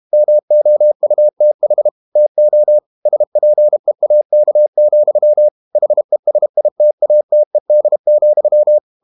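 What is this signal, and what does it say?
Morse code, heard as a keyed tone, about 32 wpm, 600 Hz. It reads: MOUTH TO SPEAK, HESITATED,
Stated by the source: Morse code practice at 32 wpm from CW med Gnister, reading Ken Follett's novel Never in English.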